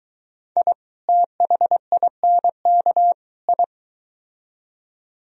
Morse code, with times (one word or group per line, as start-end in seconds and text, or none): 0.6-0.7 I
1.1-3.1 THINK
3.5-3.6 I